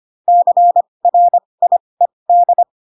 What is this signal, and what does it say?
Morse code, heard as a keyed tone, about 25 wpm, spaced as for 20 wpm, 700 Hz. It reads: CRIED
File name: Pinocchio / Pinocchio ch8 15 25 ik1hge